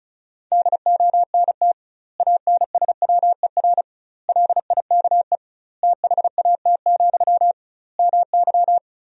Morse code, heard as a keyed tone, about 36 wpm, 700 Hz. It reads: DONT ANSWER LIKE THAT, MY
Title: 2x Word-Spacing - Sentences from Top 400 Words - 35wpm